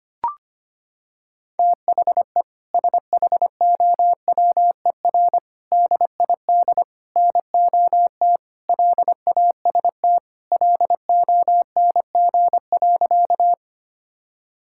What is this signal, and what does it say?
Morse code, heard as a keyed tone, about 25 wpm, 700 Hz. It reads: THE SHOWER DID NOT LAST LONG.
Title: Sentences from Top 1600 Words 25wpm